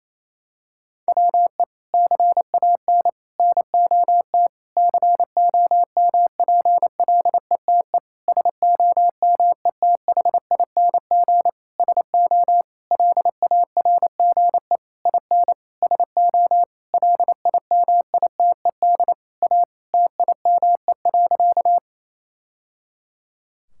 Morse code, heard as a keyed tone, about 28 wpm, 700 Hz. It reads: WE CAN NOT COMPLETE SOMETHING SO LARGE IN SO LIMITED A TIME.
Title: Sentences from Top 1000 Words - Review Part 6 - 28wpm